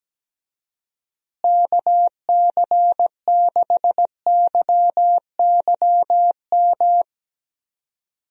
Morse code, heard as a keyed tone, about 17 wpm, 700 Hz. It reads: KC6YYM
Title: USA Call Signs - 6 Characters - Repeated 2x - 17wpm